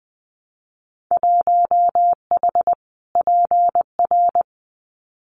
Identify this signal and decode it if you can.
Morse code, heard as a keyed tone, about 20 wpm, 700 Hz. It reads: 1H PR
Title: Winter Field Day Exchanges Encore 20wpm